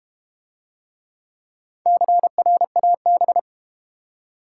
Morse code, heard as a keyed tone, about 32 wpm, 700 Hz. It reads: CRAB